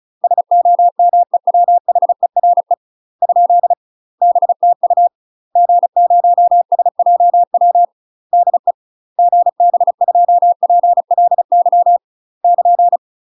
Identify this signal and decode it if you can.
Morse code, heard as a keyed tone, about 36 wpm, 700 Hz. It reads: SOMEWHERE ? BTU G0SJW DE GB2PLY <KN>
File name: UK MASTER QSOs TRAINING / 35WPM / UK MASTER QSOs 0018